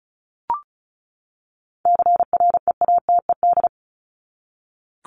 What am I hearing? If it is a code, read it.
Morse code, heard as a keyed tone, about 34 wpm, 700 Hz. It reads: CREATED
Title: ED Suffix - Top 100 Words with Base Words 35wpm